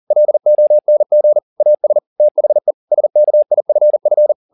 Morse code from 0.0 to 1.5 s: LONG

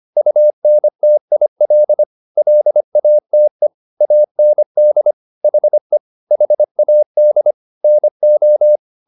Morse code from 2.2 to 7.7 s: LATE AND HE HAD

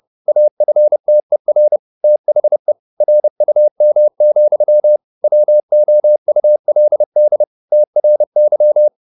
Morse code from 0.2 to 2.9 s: AFTER THE

